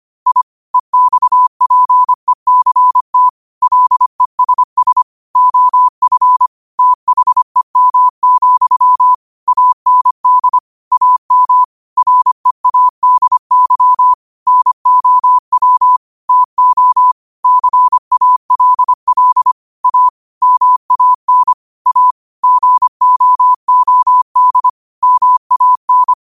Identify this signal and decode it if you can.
Morse code, heard as a keyed tone, about 25 wpm, 1000 Hz. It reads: I EXPECT LESS OF THEM, AND AM READY NOW TO CALL A MAN A GOOD MAN